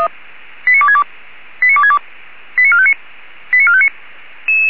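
Five-tone selective call (ZVEI), heard as quick runs of five tones, tones of about 70 ms. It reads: 89262 89262 89459 89459
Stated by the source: ZVEI five-tone call received by SDR.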